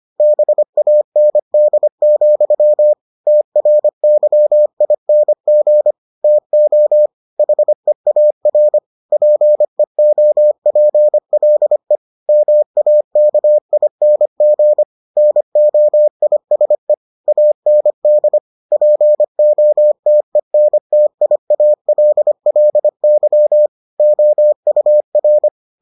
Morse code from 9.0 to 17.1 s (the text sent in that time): PEOPLE MAKING NOISE